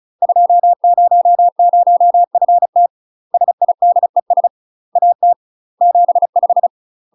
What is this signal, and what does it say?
Morse code, heard as a keyed tone, about 34 wpm, 700 Hz. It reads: 200FT SIDES AT 75
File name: UK MASTER QSOs TRAINING / 35WPM / UK MASTER QSOs 0000